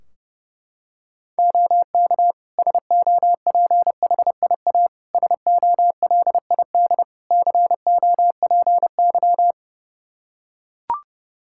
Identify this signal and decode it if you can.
Morse code, heard as a keyed tone, about 30 wpm, 700 Hz. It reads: OK SOPHIA SOLID COPY
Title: Practice QSO - Exchange 3 - More Info - 30wpm